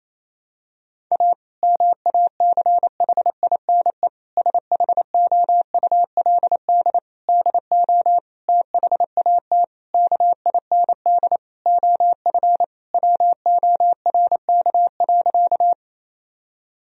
Morse code, heard as a keyed tone, about 27 wpm, 700 Hz. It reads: A MACHINE SHOULD DO THAT KIND OF WORK.
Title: Sentences from Top 1000 Words - Review Part 2 - 28wpm